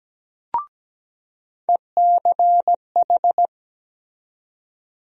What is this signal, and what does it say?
Morse code, heard as a keyed tone, about 17 wpm, 700 Hz. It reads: ECH
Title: USA Call Sign Postfixes - 3 Letters 17wpm